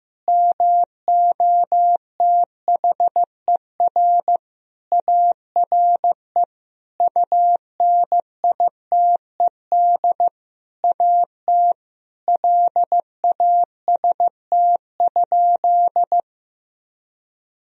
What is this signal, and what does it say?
Morse code, heard as a keyed tone, about 15 wpm, 700 Hz. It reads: MOTHER ARE UNITED AT LAST?
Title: Sentences from Top 1000 Words - Review Part 4 - 15wpm